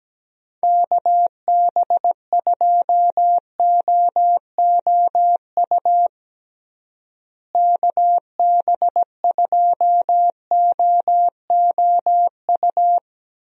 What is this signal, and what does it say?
Morse code, heard as a keyed tone, about 17 wpm, 700 Hz. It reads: KB2OOU KB2OOU
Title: USA Call Signs - 6 Characters - Repeated 2x - 17wpm